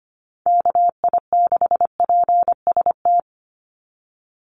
Morse code, heard as a keyed tone, about 25 wpm, 700 Hz. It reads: KI6PST
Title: USA Call Signs 25wpm